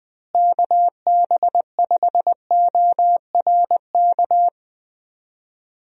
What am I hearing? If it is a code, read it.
Morse code, heard as a keyed tone, about 21 wpm, 700 Hz. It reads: KB5ORK
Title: USA Call Signs - 6 Characters - Repeated 2x - 20wpm